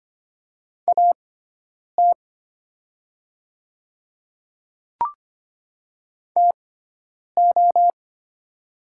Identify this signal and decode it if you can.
Morse code, heard as a keyed tone, about 25 wpm, spaced as for 9 wpm, 700 Hz. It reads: AT TO
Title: 6x Character Spacing - Top 100 Words - 25wpm